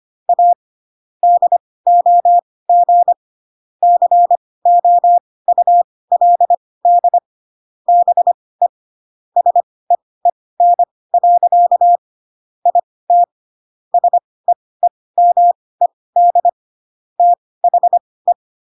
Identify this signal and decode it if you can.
Morse code, heard as a keyed tone, about 25 wpm, spaced as for 18 wpm, 700 Hz. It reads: A DOG COULD BE SEEN. IT SEEMED THE
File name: Pinocchio / Pinocchio ch6 12 25 ik1hge